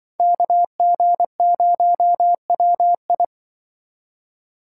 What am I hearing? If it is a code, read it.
Morse code, heard as a keyed tone, about 24 wpm, 700 Hz. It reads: KG0WI
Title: Speed Racing - USA Call Signs 20wpm